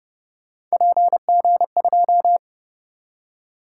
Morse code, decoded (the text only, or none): PG2